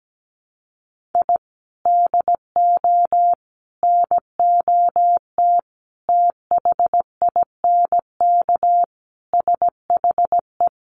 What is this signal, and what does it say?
Morse code, heard as a keyed tone, about 17 wpm, 700 Hz. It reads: I DO NOT THINK SHE
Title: Sets of 5 Words 17wpm